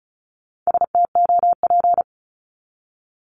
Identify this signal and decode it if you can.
Morse code, heard as a keyed tone, about 34 wpm, 700 Hz. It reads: STOP
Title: Top 500 Words 35wpm